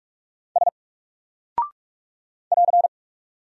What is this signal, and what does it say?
Morse code, heard as a keyed tone, about 44 wpm, 700 Hz. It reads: S <AR>
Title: Single Character (Letter:Number:Punctuation:Prosign) 45wpm